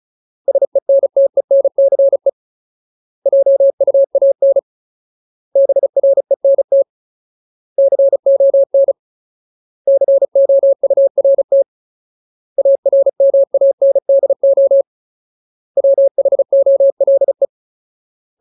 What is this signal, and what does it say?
Morse code, heard as a keyed tone, about 34 wpm, 550 Hz. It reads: SENTENCE JUAN BRENT CON COURT ARMANDO WHOLE
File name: Ultimate Word List / 35WPM Ultimate